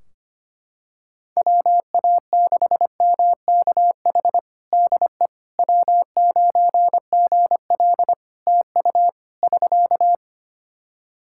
Morse code, heard as a keyed tone, about 25 wpm, 700 Hz. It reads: WA6MKH DE W9GL TU <SK>
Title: Practice QSO - Exchange 4 - Ending - 25wpm